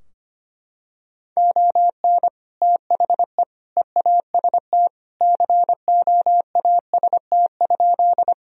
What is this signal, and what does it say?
Morse code, heard as a keyed tone, about 25 wpm, 700 Hz. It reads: ON THE EAST COAST?
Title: Sentences from Top 1000 Words - Review Part 4 - 25wpm